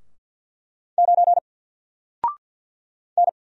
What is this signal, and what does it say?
Morse code, heard as a keyed tone, about 48 wpm, 700 Hz. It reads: 9 N
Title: Single Letter-Number Alt 50wpm